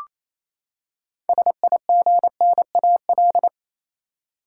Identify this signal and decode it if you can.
Morse code, heard as a keyed tone, about 28 wpm, 700 Hz. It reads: SIGNAL
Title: Six Letter Words - 28wpm